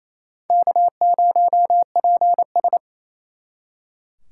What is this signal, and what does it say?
Morse code, heard as a keyed tone, about 35 wpm, 700 Hz. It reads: K0PS
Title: Speed Racing - USA 1x2 Call Signs 28wpm